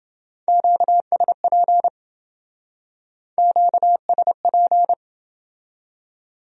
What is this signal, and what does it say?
Morse code, heard as a keyed tone, about 28 wpm, 700 Hz. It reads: QSP QSP